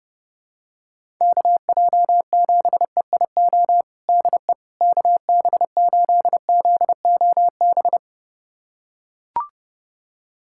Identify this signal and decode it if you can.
Morse code, heard as a keyed tone, about 30 wpm, 700 Hz. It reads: KJ7EIO DE KB8ZOB
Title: USA Callsign DE Callsign 30wpm